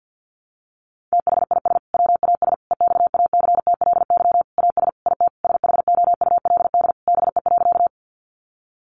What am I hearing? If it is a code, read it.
Morse code, heard as a keyed tone, about 50 wpm, 700 Hz. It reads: THIS WAS EXACTLY AS IT SHOULD BE.
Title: Sentences from Top 900 Words 50wpm